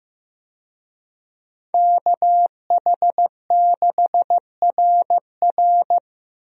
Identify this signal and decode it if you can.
Morse code, heard as a keyed tone, about 14 wpm, 700 Hz. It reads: KH6RR
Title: USA Call Signs - 5 Characters - Repeated 2x - 15wpm